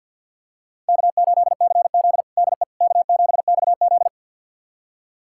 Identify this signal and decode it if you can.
Morse code, heard as a keyed tone, about 47 wpm, 700 Hz. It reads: K8CZ DE K7XZ
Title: USA Callsign DE Callsign - 4 Characters - 50wpm